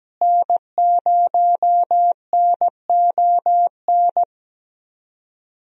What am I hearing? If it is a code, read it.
Morse code, heard as a keyed tone, about 17 wpm, 700 Hz. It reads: N0NON
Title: USA Call Signs - 5 Characters - Repeated 3x - 17wpm